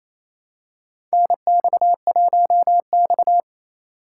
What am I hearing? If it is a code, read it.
Morse code, heard as a keyed tone, about 28 wpm, 700 Hz. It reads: NX1X
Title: USA Call Signs - 2x Character Spacing - 28wpm